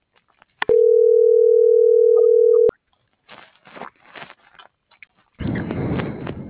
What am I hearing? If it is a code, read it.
Ringback tone (North America), 440 + 480 Hz.